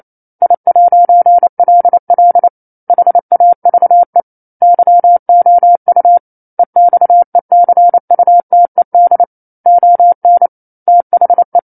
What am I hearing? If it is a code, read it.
Morse code, heard as a keyed tone, about 28 wpm, 700 Hz. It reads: I'LL HAVE YOU EXECUTED ON THE
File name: 09-Chapter-09 28 WPM